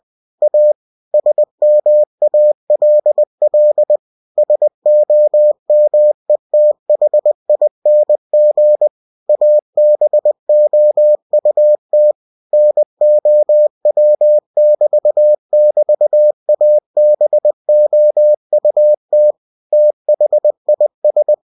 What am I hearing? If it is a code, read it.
Morse code, heard as a keyed tone, about 20 wpm, 600 Hz. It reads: A SMALL SOMETHING ABOUT NOW<BT><BT>ABOUT THIS
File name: cw-books / winnie-the-pooh / winnie-20wpm-page-0030